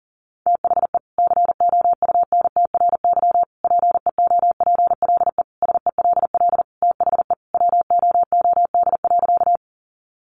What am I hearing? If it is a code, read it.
Morse code, heard as a keyed tone, about 40 wpm, 700 Hz. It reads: THE COUNTRY PEOPLE SELL THE WOOD.
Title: Sentences from Top 1100 Words 40wpm